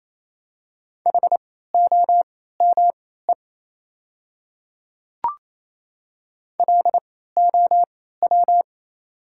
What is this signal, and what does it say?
Morse code, heard as a keyed tone, about 27 wpm, spaced as for 16 wpm, 700 Hz. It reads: HOME LOW
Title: 3x Character Spacing - Top 300 Words - 28wpm